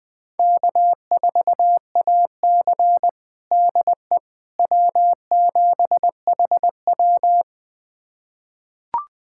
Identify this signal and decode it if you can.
Morse code, heard as a keyed tone, about 20 wpm, 700 Hz. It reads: K4AC DE W7HW
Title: USA Callsign DE Callsign - 4 Characters - 20wpm